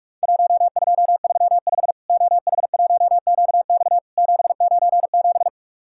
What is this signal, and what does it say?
Morse code, heard as a keyed tone, about 44 wpm, 700 Hz. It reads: 1235 OH1QX 797
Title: Worked All Europe DX Contest - QTC Exchange - 45wpm